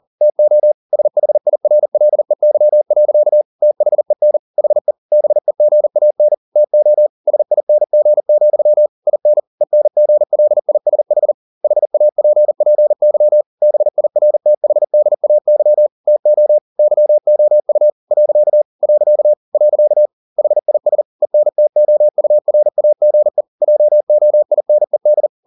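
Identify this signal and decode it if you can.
Morse code, heard as a keyed tone, about 40 wpm, 600 Hz. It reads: TO SHIRLEY. THEN HE BEGAN TO SING, IN ENGLISH HAPPY BIRTHDAY TO YOU . . HIS ENTOURAGE JOINED
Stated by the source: Morse code at 40 wpm, practice text, CW med Gnister.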